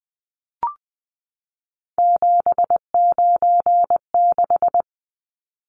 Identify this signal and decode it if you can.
Morse code, heard as a keyed tone, about 20 wpm, 700 Hz. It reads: 796
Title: Numbers - 3-Digit 20wpm